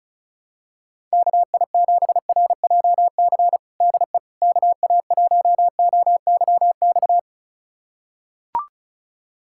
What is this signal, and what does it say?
Morse code, heard as a keyed tone, about 34 wpm, 700 Hz. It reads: KI7RJC DE KA1OYX
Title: USA Callsign DE Callsign - 6 Characters - 35wpm